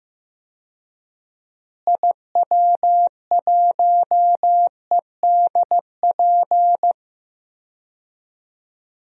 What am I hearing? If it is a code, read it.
Morse code, heard as a keyed tone, about 15 wpm, 700 Hz. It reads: IW1EDP
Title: DXCC-100 Call Signs - 6 Characters - Repeated 3x - 15wpm